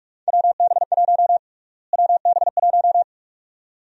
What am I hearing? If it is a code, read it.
Morse code, heard as a keyed tone, about 41 wpm, 690 Hz. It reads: WB1 WB1
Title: USA Call Sign Prefixes - Repeated 2x - 45wpm